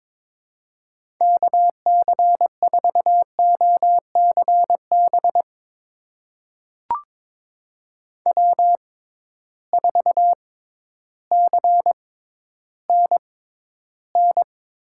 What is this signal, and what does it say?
Morse code, heard as a keyed tone, about 22 wpm, spaced as for 12 wpm, 700 Hz. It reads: KC4OCB W4CNN
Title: USA Call Signs - 6x Character Spacing - 22wpm